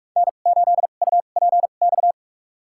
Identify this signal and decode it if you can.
Morse code, heard as a keyed tone, about 46 wpm, spaced as for 33 wpm, 700 Hz. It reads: N8UPX
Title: USA Call Signs - 5 Characters - 2x Character Spacing - 45wpm